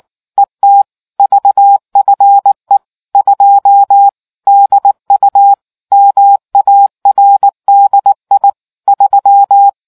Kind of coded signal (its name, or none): Morse code